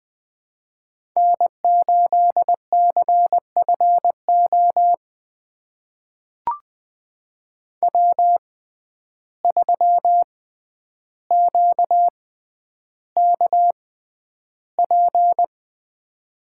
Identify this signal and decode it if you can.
Morse code, heard as a keyed tone, about 20 wpm, spaced as for 10 wpm, 700 Hz. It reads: N8CFO W3QKP
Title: USA Call Signs - 5 Characters - 6x Character Spacing - 20wpm